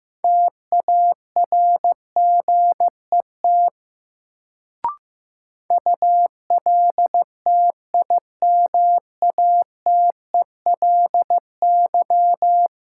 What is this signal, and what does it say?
Morse code, heard as a keyed tone, about 15 wpm, 700 Hz. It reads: TARGET ULTIMATELY
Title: Sentences from Top 1200 Words - New Word Drill - 15wpm